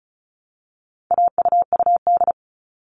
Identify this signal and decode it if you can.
Morse code, heard as a keyed tone, about 34 wpm, 700 Hz. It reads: AUUD
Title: All Cut Numbers - 4-Digit 35wpm